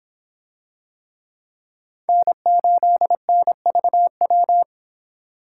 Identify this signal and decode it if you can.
Morse code, heard as a keyed tone, about 26 wpm, 700 Hz. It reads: N8NVW